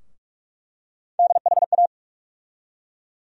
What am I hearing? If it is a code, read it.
Morse code, heard as a keyed tone, about 46 wpm, 700 Hz. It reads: DHA
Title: USA Call Sign Postfixes - 3 Letters 45wpm